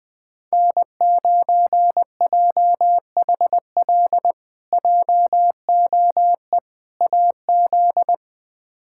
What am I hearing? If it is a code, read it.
Morse code, heard as a keyed tone, about 21 wpm, 700 Hz. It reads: N9JHL JOE AZ
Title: K1USN SST QSO Practice - 20wpm